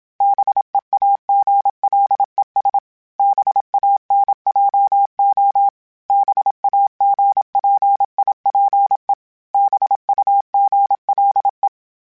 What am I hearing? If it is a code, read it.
Morse code, heard as a keyed tone, about 27 wpm, 800 Hz. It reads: BEAGLES BANJO BAGPIPE BUGLE